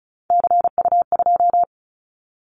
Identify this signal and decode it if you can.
Morse code, heard as a keyed tone, about 36 wpm, 700 Hz. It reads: CU2
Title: Callsign Prefixes for Top 50 DXCC - 35wpm